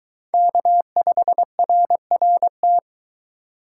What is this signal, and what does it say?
Morse code, heard as a keyed tone, about 23 wpm, 700 Hz. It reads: K5RRT